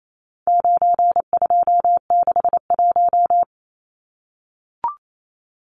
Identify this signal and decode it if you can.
Morse code, heard as a keyed tone, about 28 wpm, 700 Hz. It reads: 9261
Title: Numbers - 4-Digit 28wpm